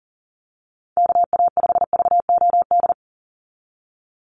Morse code, heard as a keyed tone, about 40 wpm, 700 Hz. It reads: KA5VOD
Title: USA Call Signs 40wpm